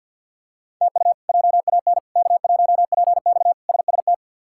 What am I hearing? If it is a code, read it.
Morse code, heard as a keyed tone, about 47 wpm, 690 Hz. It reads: TU JAN K1PX SST
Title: K1USN SST QSO Practice - 50wpm